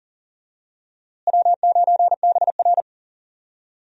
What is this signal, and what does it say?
Morse code, heard as a keyed tone, about 40 wpm, 700 Hz. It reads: W9BR
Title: USA Call Signs - 4 Characters - Repeated 3x - 40wpm